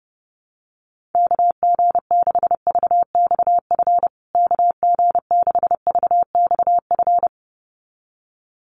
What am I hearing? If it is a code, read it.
Morse code, heard as a keyed tone, about 30 wpm, 700 Hz. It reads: KG6VXF KG6VXF
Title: USA Call Signs Repeated 2x 30wpm